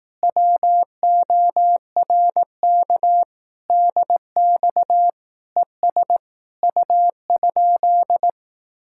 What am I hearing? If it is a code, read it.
Morse code, heard as a keyed tone, about 18 wpm, 700 Hz. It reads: WORK DX ES U?
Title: Speed-Racing Common QSO Phrases Encore - 15wpm